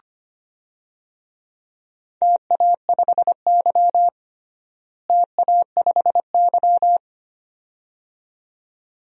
Morse code, heard as a keyed tone, about 25 wpm, 700 Hz. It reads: TA5Y TA5Y
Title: DXCC-100 Call Signs - 4 Characters - Repeated 2x - 25wpm